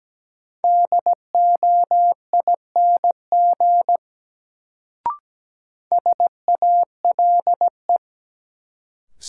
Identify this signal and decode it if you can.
Morse code, heard as a keyed tone, about 17 wpm, 700 Hz. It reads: DOING SALE
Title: Sentences from Top 1400 Words - New Word Drill - 17wpm